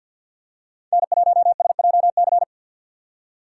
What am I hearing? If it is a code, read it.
Morse code, heard as a keyed tone, about 50 wpm, 700 Hz. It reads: N1SJC
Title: USA Call Signs - 2x Character Spacing - 50wpm